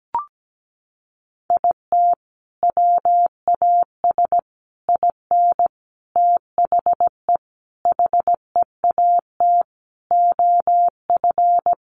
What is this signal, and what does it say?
Morse code, heard as a keyed tone, about 17 wpm, 700 Hz. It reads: IT WAS IN THE HEAT OF
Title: Sentences from Top 1300 Words 17wpm